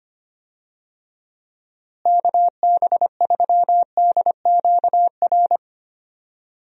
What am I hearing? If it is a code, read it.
Morse code, heard as a keyed tone, about 25 wpm, 700 Hz. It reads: KB3DQR
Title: USA Call Signs - 6 Characters - Repeated 3x - 25wpm